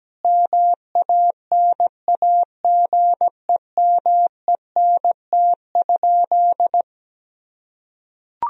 Morse code, heard as a keyed tone, about 17 wpm, 700 Hz. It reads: MANAGEMENT?
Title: Sentences from Top 1000 Words - Review Part 4 - 17wpm